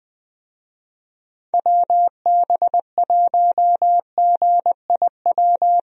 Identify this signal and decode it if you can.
Morse code, heard as a keyed tone, about 20 wpm, 700 Hz. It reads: WB1GIW
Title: USA Call Signs - 6 Characters - Repeated 3x - 20wpm